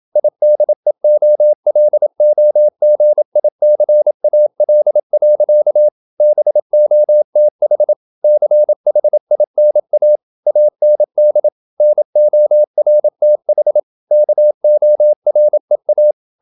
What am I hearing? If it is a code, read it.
Morse code, heard as a keyed tone, about 27 wpm, 600 Hz. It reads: IDEOLOGICAL. BOTH CHINA AND NORTH KOREA
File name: Never 0027wpm